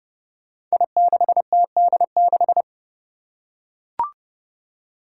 Morse code, heard as a keyed tone, about 30 wpm, 700 Hz. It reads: I6TD6
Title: Five Challenging Letter-Number Combinations 30wpm